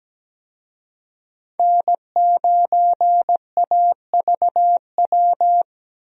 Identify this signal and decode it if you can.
Morse code, heard as a keyed tone, about 17 wpm, 700 Hz. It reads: N9AVW
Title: USA Call Signs - 5 Characters - 17wpm